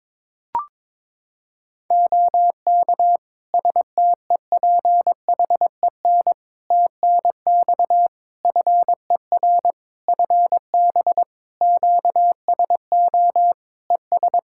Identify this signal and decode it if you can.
Morse code, heard as a keyed tone, about 22 wpm, 700 Hz. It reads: OK STEPHEN TNX FER FB QSO ES
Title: K1USN SST QSO Practice 22wpm